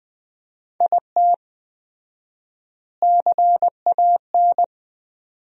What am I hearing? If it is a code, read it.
Morse code, heard as a keyed tone, about 20 wpm, 700 Hz. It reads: IT CAN
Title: 4x Word-Spacing - Sentences from Top 400 Words - 20wpm